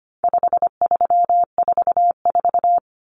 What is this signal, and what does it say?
Morse code, heard as a keyed tone, about 25 wpm, 700 Hz. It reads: 5344
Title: Numbers - 4-Digit 25wpm